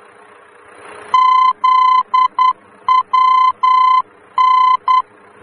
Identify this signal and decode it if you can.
Morse code, heard as a keyed tone, about 10 wpm, 1020 Hz. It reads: ZWN
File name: germany / mx / ZWN-D-VOR